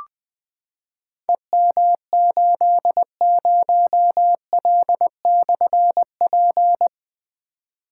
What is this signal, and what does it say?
Morse code, heard as a keyed tone, about 21 wpm, 700 Hz. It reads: EM80L/P
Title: Difficult Call Signs 20wpm